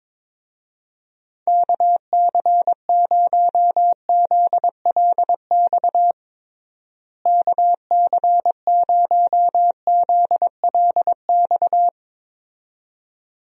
Morse code, heard as a keyed tone, about 22 wpm, 700 Hz. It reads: KC0ZLX KC0ZLX
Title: USA Call Signs - 6 Characters - Repeated 2x - 22wpm